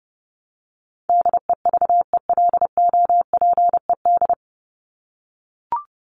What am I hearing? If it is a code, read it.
Morse code, heard as a keyed tone, about 30 wpm, 700 Hz. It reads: DEVELOPED
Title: ED Suffix - Top 100 Words with Base Words 30wpm